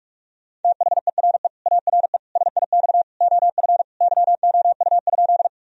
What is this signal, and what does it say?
Morse code, heard as a keyed tone, about 43 wpm, 700 Hz. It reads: THERE ARE SIX OF YOU?
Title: Sentences from Top 1000 Words - Review Part 2 - 45wpm